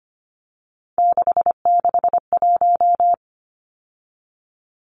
Morse code, heard as a keyed tone, about 25 wpm, 700 Hz. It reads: 661
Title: Numbers - 3-Digit 25wpm